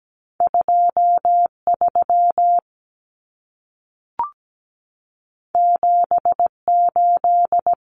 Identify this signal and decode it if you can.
Morse code, heard as a keyed tone, about 17 wpm, 700 Hz. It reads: 23 78
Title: Common Cut Numbers - 2-Digit 17wpm